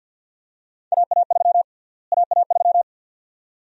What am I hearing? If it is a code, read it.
Morse code, heard as a keyed tone, about 46 wpm, 700 Hz. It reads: AA3 AA3
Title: USA Call Sign Prefixes - Repeated 2x - 50wpm